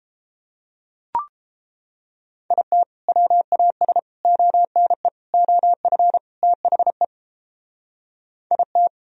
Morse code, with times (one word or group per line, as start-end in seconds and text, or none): 2.5-2.8 IT
3.1-4.0 WAS
4.2-5.1 ONE
5.3-6.2 OF
6.4-7.0 THE
8.5-8.9 IT